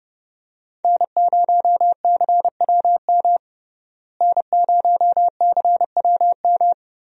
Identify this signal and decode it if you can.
Morse code, heard as a keyed tone, about 32 wpm, 700 Hz. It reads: N0CWM N0CWM
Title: USA Call Signs - 5 Characters - Repeated 3x - 30wpm